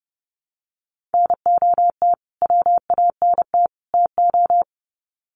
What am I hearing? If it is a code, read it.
Morse code, heard as a keyed tone, about 30 wpm, 700 Hz. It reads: NOT WANT TO
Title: Sets of 3 Words 30wpm